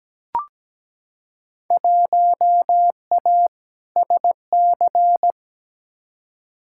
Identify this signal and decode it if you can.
Morse code, heard as a keyed tone, about 17 wpm, 700 Hz. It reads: 1A SC
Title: Field Day Exchanges - 17wpm